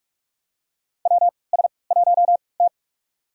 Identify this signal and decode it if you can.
Morse code, heard as a keyed tone, about 46 wpm, spaced as for 26 wpm, 700 Hz. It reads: WS1T
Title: USA Call Signs - 4 Characters - 3x Character Spacing - 45wpm